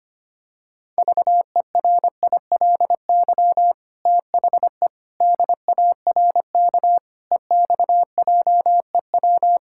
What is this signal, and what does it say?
Morse code, heard as a keyed tone, about 25 wpm, 700 Hz. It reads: VERILY THE DARK EXJEW
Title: Pangram Challenge v2 - 25wpm